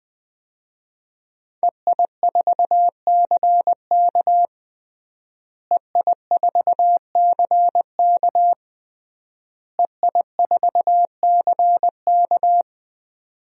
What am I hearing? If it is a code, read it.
Morse code, heard as a keyed tone, about 20 wpm, 700 Hz. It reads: EI4CK EI4CK EI4CK
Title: DXCC-100 Call Signs - 5 Characters - Repeated 3x - 20wpm